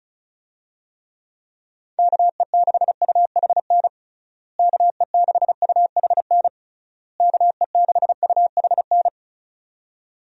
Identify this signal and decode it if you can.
Morse code, heard as a keyed tone, about 34 wpm, 700 Hz. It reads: KE6UHN KE6UHN KE6UHN
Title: USA Call Signs - 6 Characters - Repeated 3x - 35wpm